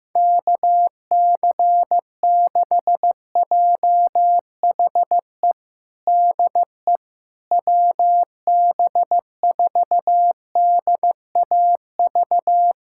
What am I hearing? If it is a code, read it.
Morse code, heard as a keyed tone, about 15 wpm, 700 Hz. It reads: KC6JHE DE WB4DAV
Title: USA Callsign DE Callsign 15wpm